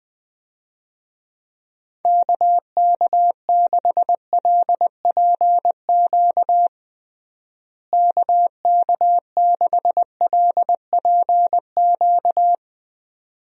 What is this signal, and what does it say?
Morse code, heard as a keyed tone, about 21 wpm, 700 Hz. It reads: KK6LPQ KK6LPQ
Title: USA Call Signs - 6 Characters - Repeated 2x - 20wpm